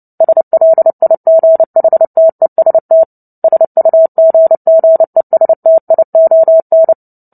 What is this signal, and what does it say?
Morse code, heard as a keyed tone, about 30 wpm, 650 Hz. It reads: SLIGHTEST SUGGESTION
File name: Book Two / 03-Chapter 3 - 29 WPM